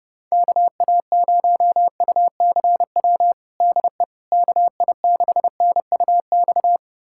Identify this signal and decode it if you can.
Morse code, heard as a keyed tone, about 32 wpm, 700 Hz. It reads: KA0UCW DE KI6NUX
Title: USA Callsign DE Callsign - 6 Characters - 30wpm